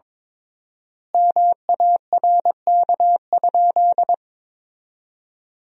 Morse code, heard as a keyed tone, about 22 wpm, 700 Hz. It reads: MARK?
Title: 3x Word-Spacing - Sentences from Top 300 Words - 22wpm